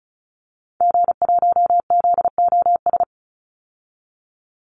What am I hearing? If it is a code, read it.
Morse code, heard as a keyed tone, about 34 wpm, 700 Hz. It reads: G1ZOS
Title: Call Signs - 35wpm